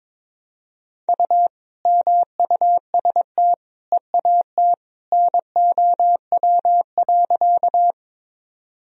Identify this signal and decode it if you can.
Morse code, heard as a keyed tone, about 22 wpm, 700 Hz. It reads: U MUST EAT NOW.
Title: Sentences from Top 300 Words and Abbreviations 22wpm